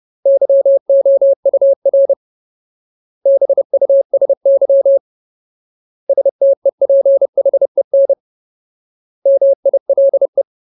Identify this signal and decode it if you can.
Morse code, heard as a keyed tone, about 30 wpm, 550 Hz. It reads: YOUR BUSY STEPHEN MILE
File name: Ultimate Word List / 30WPM Ultimate